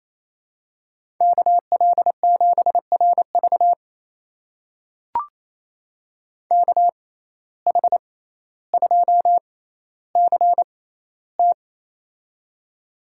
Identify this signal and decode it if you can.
Morse code, heard as a keyed tone, about 28 wpm, spaced as for 14 wpm, 700 Hz. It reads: KL7RV KH2CT
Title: USA Call Signs - 5 Characters - 6x Character Spacing - 28wpm